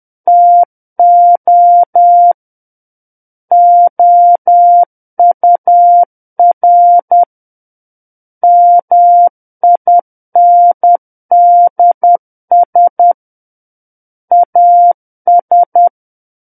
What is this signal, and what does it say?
Morse code, heard as a keyed tone, about 10 wpm, 700 Hz. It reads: TO OUR MINDS AS
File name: Book One / 01-Chapter 1 - 10 WPM